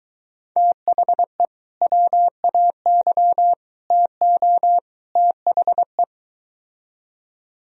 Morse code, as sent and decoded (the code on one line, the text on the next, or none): - .... / .-- .- -.-- / - --- / - .... .
THE WAY TO THE